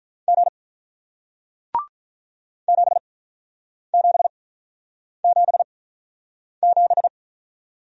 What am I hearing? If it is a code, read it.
Morse code, heard as a keyed tone, about 46 wpm, spaced as for 9 wpm, 700 Hz. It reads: G 7777